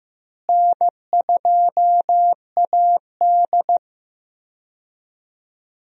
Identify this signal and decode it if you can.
Morse code, heard as a keyed tone, about 15 wpm, 700 Hz. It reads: N2AD
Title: USA Call Signs - 4 Characters - Repeated 3x - 15wpm